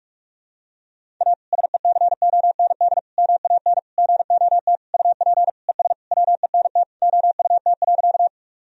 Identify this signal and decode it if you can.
Morse code, heard as a keyed tone, about 45 wpm, 700 Hz. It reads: A SECOND MAN GOT UP ES WENT OUT.
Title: Sentences from Top 300 Words and Abbreviations 45wpm